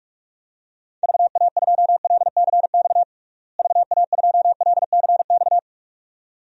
Morse code, heard as a keyed tone, about 42 wpm, 700 Hz. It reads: VA2LCX VA2LCX
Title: DXCC-100 Call Signs - 6 Characters - Repeated 2x - 45wpm